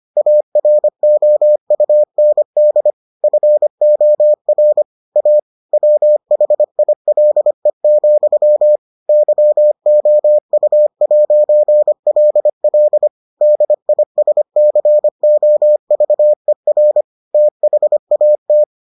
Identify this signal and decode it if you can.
Morse code, heard as a keyed tone, about 25 wpm, 600 Hz. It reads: AROUND FOR A WHILE, YOU'LL DISCOVER THAT